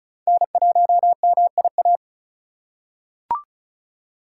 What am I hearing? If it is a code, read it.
Morse code, heard as a keyed tone, about 36 wpm, 700 Hz. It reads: N1MIA
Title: USA Call Signs - 3x Character Spacing - 35wpm